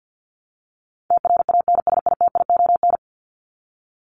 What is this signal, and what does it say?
Morse code, heard as a keyed tone, about 49 wpm, 700 Hz. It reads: TRANSITION
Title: Top 1751-2000 Words 50wpm